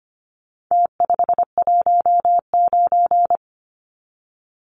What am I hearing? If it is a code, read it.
Morse code, heard as a keyed tone, about 25 wpm, 700 Hz. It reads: T519
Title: Numbers - Contest Style - 4-Digit 25wpm